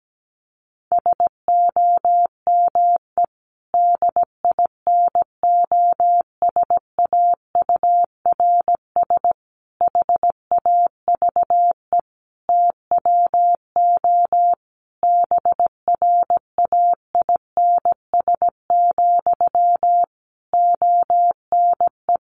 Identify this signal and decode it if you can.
Morse code, heard as a keyed tone, about 17 wpm, 700 Hz. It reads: SOME DINOSAURS HAVE TWO BRAINS, ONE
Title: A Sound Of Thunder 17wpm